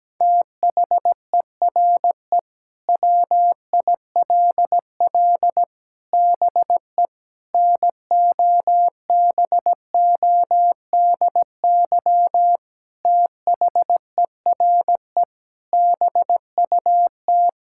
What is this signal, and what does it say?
Morse code, heard as a keyed tone, about 17 wpm, 700 Hz. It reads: THERE WILL BE NOBODY THERE BUT
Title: Sentences from Top 1600 Words 17wpm